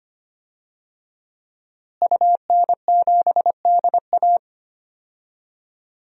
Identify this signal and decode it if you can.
Morse code, heard as a keyed tone, about 25 wpm, 700 Hz. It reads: UN7DA
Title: DXCC-100 Call Signs - 5 Characters - Repeated 2x - 25wpm